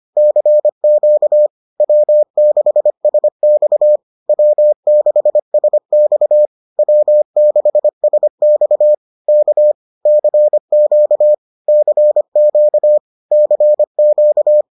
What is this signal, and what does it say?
Morse code, heard as a keyed tone, about 25 wpm, 600 Hz. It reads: CQ W6SX W6SX W6SX K CQ CQ CQ